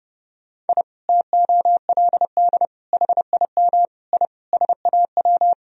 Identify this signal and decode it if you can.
Morse code, heard as a keyed tone, about 30 wpm, 700 Hz. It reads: I TOLD HIM I SAW
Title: Sentences from Top 1000 Words - Review Part 1 - 30wpm